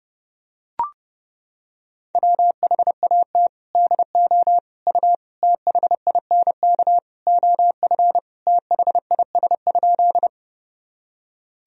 Morse code, heard as a keyed tone, about 30 wpm, 700 Hz. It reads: WHAT DO U THINK OF THIS?
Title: Sentences from Top 200 Words and Abbreviations 30wpm